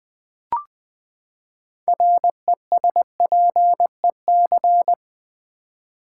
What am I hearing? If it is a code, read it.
Morse code, heard as a keyed tone, about 20 wpm, 700 Hz. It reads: RESPEC
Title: Build A Word - 6 Letters At A Time - 12-Letter Words - 20wpm